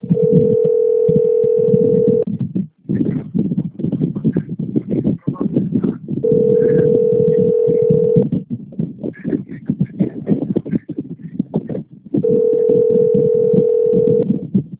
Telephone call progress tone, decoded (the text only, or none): ringback tone